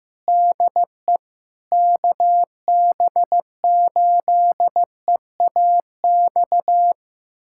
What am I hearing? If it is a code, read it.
Morse code, heard as a keyed tone, about 15 wpm, 700 Hz. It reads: DE KB8EAX
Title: USA Callsign DE Callsign - 6 Characters - 15wpm